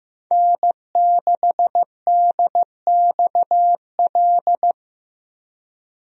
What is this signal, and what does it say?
Morse code, heard as a keyed tone, about 15 wpm, 700 Hz. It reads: N6DXL
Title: USA Call Signs - 5 Characters - 15wpm